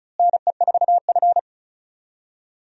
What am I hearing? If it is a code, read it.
Morse code, heard as a keyed tone, about 36 wpm, 700 Hz. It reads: NE4F
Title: USA Call Signs - 4 Characters - Repeated 3x - 35wpm